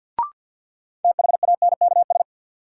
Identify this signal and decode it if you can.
Morse code, heard as a keyed tone, about 47 wpm, 700 Hz. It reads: THANKS
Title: Sets of 5 Words - New Word Drill - 50wpm